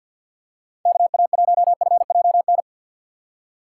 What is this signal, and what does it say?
Morse code, heard as a keyed tone, about 52 wpm, 700 Hz. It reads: KA1FJN
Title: USA Call Signs - 6 Characters - 3x Character Spacing - 50wpm